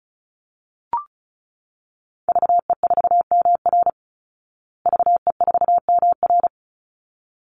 Morse code, heard as a keyed tone, about 34 wpm, 700 Hz. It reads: VE4MR VE4MR
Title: Call Signs Repeated 3x - 35wpm